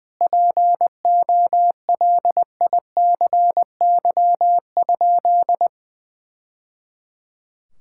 Morse code, heard as a keyed tone, about 21 wpm, 700 Hz. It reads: POLICY?
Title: Sentences from Top 1000 Words - Review Part 5 - 20wpm